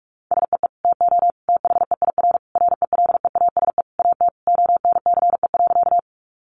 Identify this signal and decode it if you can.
Morse code, heard as a keyed tone, about 44 wpm, 700 Hz. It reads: SEE TO THEIR RELEASE AT ONCE.
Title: Sentences from Top 1100 Words 45wpm